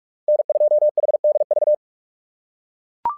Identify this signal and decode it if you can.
Morse code, heard as a keyed tone, about 44 wpm, 600 Hz. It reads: N2HDV
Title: SOTA QSO Practice 45wpm